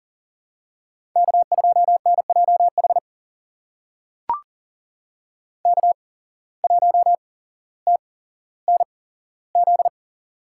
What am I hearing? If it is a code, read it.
Morse code, heard as a keyed tone, about 39 wpm, spaced as for 17 wpm, 700 Hz. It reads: K2NJH K1TNZ
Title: USA Call Signs - 5 Characters - 8x Character Spacing - 40wpm